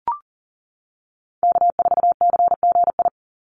Morse code, heard as a keyed tone, about 40 wpm, 700 Hz. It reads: K4CGI
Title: USA Call Signs 40wpm